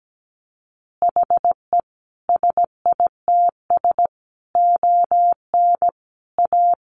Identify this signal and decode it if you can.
Morse code, heard as a keyed tone, about 17 wpm, 700 Hz. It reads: HE SITS ON A
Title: Additional Easy Phrases 17wpm